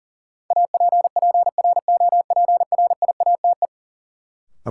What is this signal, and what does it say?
Morse code, heard as a keyed tone, about 40 wpm, 700 Hz. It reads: APPROPRIATE
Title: Sentences from Top 900 Words - New Word Drill - 40wpm